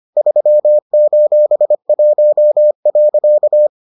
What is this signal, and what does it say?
Morse code, heard as a keyed tone, about 25 wpm, 600 Hz. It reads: 3:1.